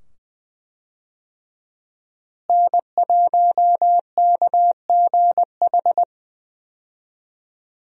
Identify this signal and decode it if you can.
Morse code, heard as a keyed tone, about 20 wpm, 700 Hz. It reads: N1KGH
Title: USA Call Signs - 5 Characters - Repeated 2x - 20wpm